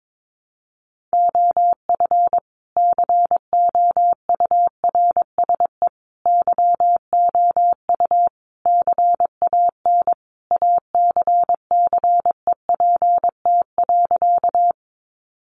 Morse code, heard as a keyed tone, about 22 wpm, 700 Hz. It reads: OF COURSE YOU CAN ACCEPT.
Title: Sentences from Top 1300 Words 22wpm